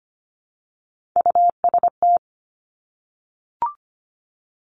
Morse code, heard as a keyed tone, about 25 wpm, 700 Hz. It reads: UST
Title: Top Three Letter Combinations in Words 25wpm